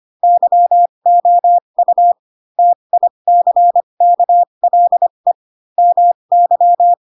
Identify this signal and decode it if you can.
Morse code, heard as a keyed tone, about 25 wpm, spaced as for 22 wpm, 700 Hz. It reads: YOU TICKLE MY